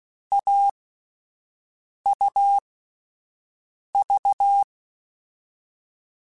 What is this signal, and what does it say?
Morse code, heard as a keyed tone, about 16 wpm, 800 Hz. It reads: A U V
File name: Difficult Combos / Difficult Combos-mp3s / AUV AUV4 @10